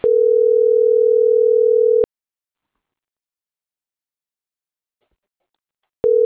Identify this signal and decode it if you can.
Ringback tone (North America), 440 + 480 Hz.